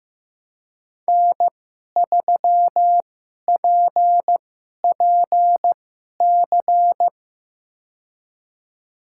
Morse code, heard as a keyed tone, about 15 wpm, spaced as for 11 wpm, 700 Hz. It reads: N3PPC
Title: USA Call Signs - 5 Characters - 2x Character Spacing - 15wpm